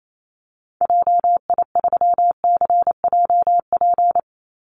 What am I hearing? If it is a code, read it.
Morse code, heard as a keyed tone, about 28 wpm, 700 Hz. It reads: JI3CJP
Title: Call Signs Repeated 3x - 28wpm